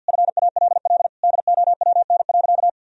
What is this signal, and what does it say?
Morse code, heard as a keyed tone, about 50 wpm, 700 Hz. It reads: FALL DOWN.
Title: Sentences from Top 1000 Words - Review Part 2 - 50wpm